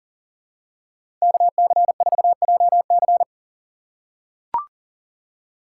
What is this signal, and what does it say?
Morse code, heard as a keyed tone, about 39 wpm, 700 Hz. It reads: KC4JC
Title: USA Call Signs - 5 Characters - 3x Character Spacing - 40wpm